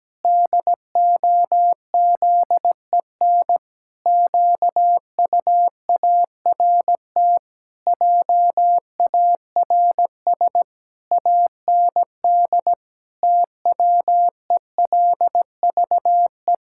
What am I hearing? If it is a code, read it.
Morse code, heard as a keyed tone, about 17 wpm, 700 Hz. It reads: DOZEN QUART JARS AND TWELVE
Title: Pangram Challenge v2 - 17wpm